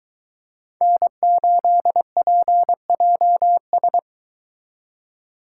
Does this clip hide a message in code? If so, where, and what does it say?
Yes: Morse code from 0.8 to 4.0 s, heard as a keyed tone, about 22 wpm, 700 Hz. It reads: N8PJS